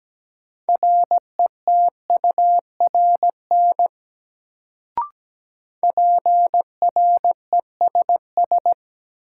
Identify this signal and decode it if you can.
Morse code, heard as a keyed tone, about 17 wpm, 700 Hz. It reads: RETURN PRESS
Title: Sentences from Top 800 Words - New Word Drill - 17wpm